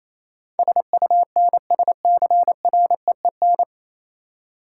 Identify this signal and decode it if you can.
Morse code, heard as a keyed tone, about 27 wpm, 700 Hz. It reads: SUNSCREEN
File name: Easy Words - Speed-Racing with Hints - 22wpm